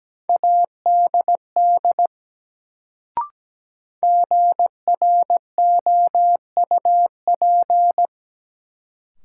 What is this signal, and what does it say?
Morse code, heard as a keyed tone, about 17 wpm, 700 Hz. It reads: ADD GROUP
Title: Sentences from Top 1000 Words - Review Part 1 - New Word Drill - 17wpm